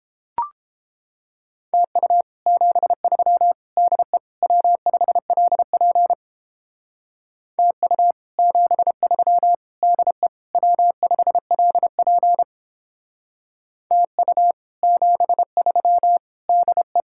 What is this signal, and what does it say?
Morse code, heard as a keyed tone, about 31 wpm, 700 Hz. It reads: TU 73 DE W5LP TU 73 DE W5LP TU 73 DE